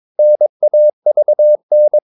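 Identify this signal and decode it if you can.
Morse code, heard as a keyed tone, about 22 wpm, 600 Hz. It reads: NAVN
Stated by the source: Morse practice of short Danish words at 22 wpm, from CW med Gnister.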